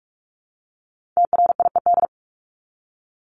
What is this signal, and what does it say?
Morse code, heard as a keyed tone, about 44 wpm, 700 Hz. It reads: TRIED
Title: ED Suffix - Top 100 Words with Base Words 45wpm